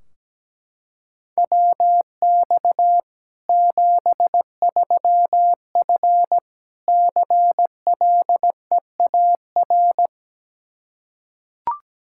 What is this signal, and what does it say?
Morse code, heard as a keyed tone, about 17 wpm, 700 Hz. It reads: WX 73F CLEAR
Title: Practice QSO - Exchange 2 - Info - 17wpm